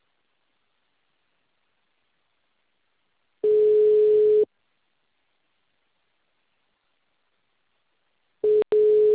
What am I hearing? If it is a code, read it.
Ringback tone (Europe, 425 Hz).
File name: out- -2044- - - .42776